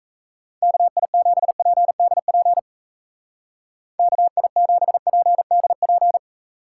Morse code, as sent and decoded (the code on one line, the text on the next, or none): -.- .. --... .--. -.. .--. / -.- .. --... .--. -.. .--.
KI7PDP KI7PDP